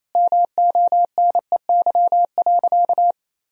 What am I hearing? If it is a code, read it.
Morse code, heard as a keyed tone, about 28 wpm, 700 Hz. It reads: MONEY.
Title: Sentences from Top 500 Words and Abbreviations 28wpm